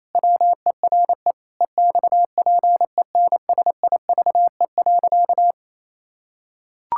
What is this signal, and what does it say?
Morse code, heard as a keyed tone, about 27 wpm, 700 Hz. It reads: WERE EXPENSIVE.
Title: Pangram Challenge v2 - 28wpm